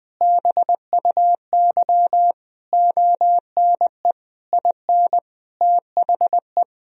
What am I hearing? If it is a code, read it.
Morse code, heard as a keyed tone, about 20 wpm, 700 Hz. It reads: BUY ONE IN THE
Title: Sentences from Top 1000 Words - Review Part 4 - 20wpm